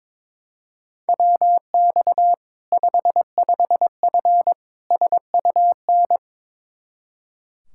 Morse code, heard as a keyed tone, about 22 wpm, 700 Hz. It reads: WX 55F SUN
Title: Practice QSO - Exchange 2 - Info - 22wpm